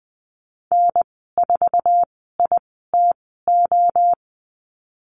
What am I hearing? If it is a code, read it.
Morse code, heard as a keyed tone, about 20 wpm, spaced as for 15 wpm, 700 Hz. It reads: N4ITO
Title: 048 Lesson 9g - Number 4 - Call Signs 20wpm